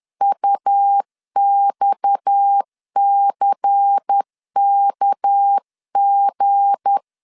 Morse code, heard as a keyed tone, about 10 wpm, 800 Hz. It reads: UXCKG
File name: LEZIONI / 7 GF9Z / 7 GF 10wpm 2514